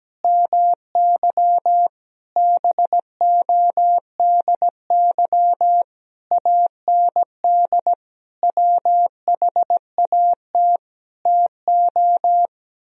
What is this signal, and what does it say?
Morse code, heard as a keyed tone, about 17 wpm, 700 Hz. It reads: MY BODY AND WHAT TO
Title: Sentences from Top 1000 Words - Review Part 2 - 17wpm